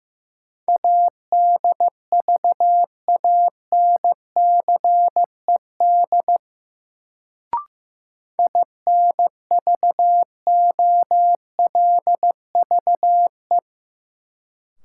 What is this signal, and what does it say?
Morse code, heard as a keyed tone, about 15 wpm, 700 Hz. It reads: ADVANCED INVOLVE
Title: Sentences from Top 1200 Words - New Word Drill - 15wpm